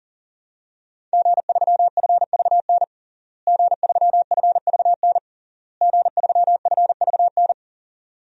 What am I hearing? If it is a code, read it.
Morse code, heard as a keyed tone, about 40 wpm, 700 Hz. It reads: G3FVN G3FVN G3FVN
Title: DXCC-100 Call Signs - 5 Characters - Repeated 3x - 40wpm